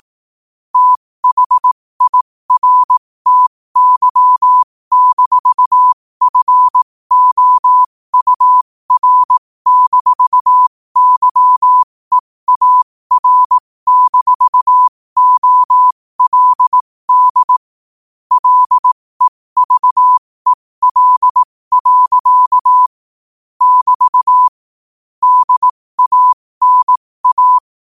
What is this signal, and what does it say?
Morse code, heard as a keyed tone, about 18 wpm, spaced as for 15 wpm, 1000 Hz. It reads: THIRTY-FOUR-YEAR-OLD LEVEL. <BT> DANA
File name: qotdfiles / 15wpm